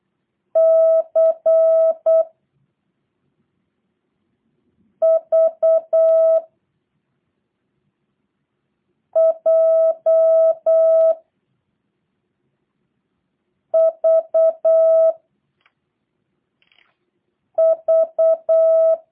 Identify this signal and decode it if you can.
Morse code, heard as a keyed tone, about 8 wpm, 660 Hz. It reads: C V J V V
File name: jh8cbh / 2023cw / review1-5